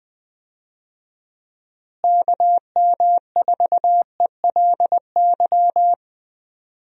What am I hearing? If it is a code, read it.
Morse code, heard as a keyed tone, about 20 wpm, 700 Hz. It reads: KM4ELY
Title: USA Call Signs - 6 Characters - Repeated 2x - 20wpm